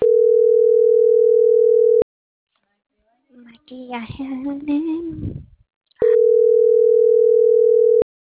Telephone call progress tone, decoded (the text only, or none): ringback tone